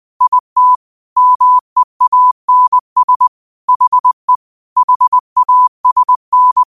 Morse code, heard as a keyed tone, about 20 wpm, 1000 Hz. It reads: IT MEANS HE HASN